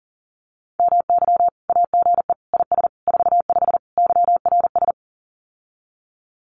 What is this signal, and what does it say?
Morse code, heard as a keyed tone, about 40 wpm, 700 Hz. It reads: MY AGE IS 45 YRS
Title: Common QSO Phrases Encore 40wpm